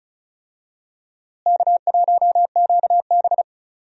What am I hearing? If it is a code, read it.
Morse code, heard as a keyed tone, about 36 wpm, 700 Hz. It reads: K1QB
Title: USA Call Signs - 4 Characters - Repeated 2x - 35wpm